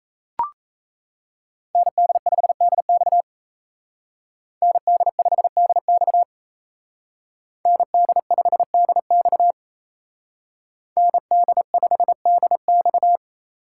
Morse code, heard as a keyed tone, about 33 wpm, 700 Hz. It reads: ND5DX ND5DX ND5DX ND5DX
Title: Speed Racing - USA 2x2 Call Signs 28wpm